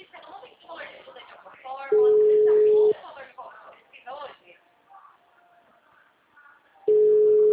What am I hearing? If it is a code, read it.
Ringback tone (Europe, 425 Hz), with speech.